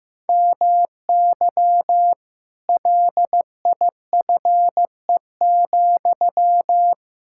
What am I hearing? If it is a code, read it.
Morse code, heard as a keyed tone, about 15 wpm, 700 Hz. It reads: MY LIFE,
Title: Sentences from Top 1600 Words 15wpm